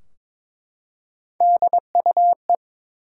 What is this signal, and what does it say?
Morse code, heard as a keyed tone, about 22 wpm, 700 Hz. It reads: DUE
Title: Three Letter Words - 22wpm